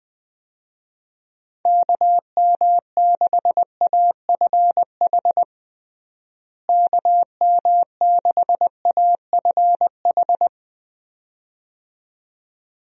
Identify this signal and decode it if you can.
Morse code, heard as a keyed tone, about 21 wpm, 700 Hz. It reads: KM6AFH KM6AFH
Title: USA Call Signs - 6 Characters - Repeated 2x - 20wpm